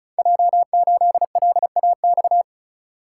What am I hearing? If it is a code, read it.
Morse code, heard as a keyed tone, about 35 wpm, 700 Hz. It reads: J8LAX
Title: Code Groups - 5 Letter:Numbers - 35wpm